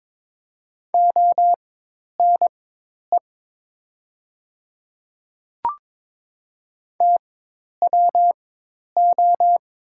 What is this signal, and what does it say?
Morse code, heard as a keyed tone, about 22 wpm, spaced as for 10 wpm, 700 Hz. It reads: ONE TWO
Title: 4x Character Spacing - Top 100 Words - 22wpm